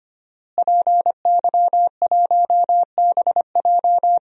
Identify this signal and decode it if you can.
Morse code, heard as a keyed tone, about 25 wpm, 700 Hz. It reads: PY1BJ
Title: DXCC-100 Call Signs - 5 Characters - Repeated 3x - 25wpm